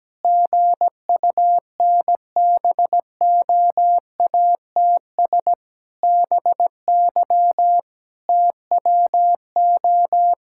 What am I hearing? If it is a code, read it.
Morse code, heard as a keyed tone, about 17 wpm, 700 Hz. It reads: GUNBOATS BY TWO
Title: Pangram Challenge v2 - 17wpm